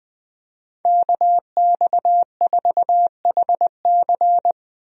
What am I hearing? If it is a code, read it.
Morse code, heard as a keyed tone, about 21 wpm, 700 Hz. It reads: KX4HC
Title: USA Call Signs - 5 Characters - Repeated 2x - 20wpm